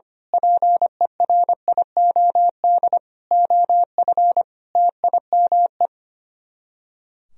Morse code, heard as a keyed tone, about 24 wpm, 700 Hz. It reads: PERIOD OF TIME
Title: Speed-Racing - Sets of 5 Words 25wpm